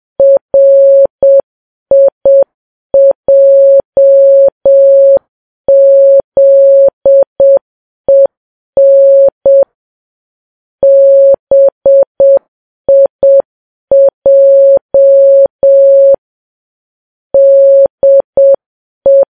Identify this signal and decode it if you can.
Morse code, heard as a keyed tone, about 7 wpm, 550 Hz. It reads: RIJZEN BIJ DE
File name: Files Web tls / Morse on9fv / morse7kps